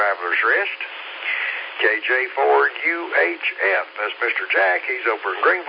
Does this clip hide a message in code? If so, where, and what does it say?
Yes: Emergency Alert System (SAME) tones from 1.2 to 4.5 s, heard as harsh buzzing bursts, with speech.